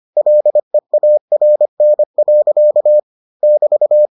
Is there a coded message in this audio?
Yes: Morse code from 0.2 to 4.1 s, heard as a keyed tone, about 25 wpm, 600 Hz. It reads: LEARN. <BT>